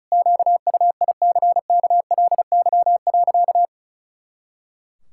Morse code, heard as a keyed tone, about 34 wpm, 700 Hz. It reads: QUICKLY.
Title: Pangram Challenge v2 - 35wpm